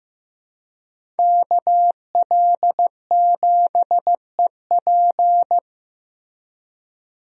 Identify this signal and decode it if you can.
Morse code, heard as a keyed tone, about 15 wpm, 700 Hz. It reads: KL7EP
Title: DXCC-100 Call Signs - 5 Characters - 15wpm